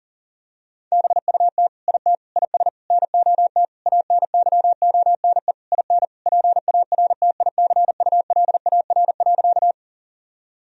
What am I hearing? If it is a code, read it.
Morse code, heard as a keyed tone, about 40 wpm, 700 Hz. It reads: BUT IT IS NOT ANYONE IN PARTICULAR.
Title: Sentences from Top 1000 Words - Review Part 4 - 40wpm